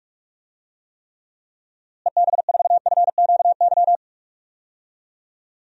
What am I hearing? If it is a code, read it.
Morse code, heard as a keyed tone, about 46 wpm, 700 Hz. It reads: EB4FQY
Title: DXCC-100 Call Signs - 6 Characters - Repeated 3x - 45wpm